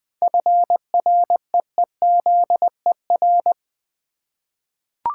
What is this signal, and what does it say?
Morse code, heard as a keyed tone, about 20 wpm, 700 Hz. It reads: FREEZER
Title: Easy Words with Hints - 20wpm